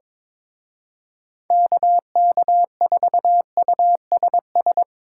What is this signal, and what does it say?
Morse code, heard as a keyed tone, about 24 wpm, 700 Hz. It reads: KK4USS
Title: Speed Racing - USA 2x3 Call Signs 22wpm